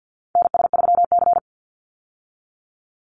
Morse code, heard as a keyed tone, about 50 wpm, 700 Hz. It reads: NS3C
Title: Call Signs Repeated 2x - 50wpm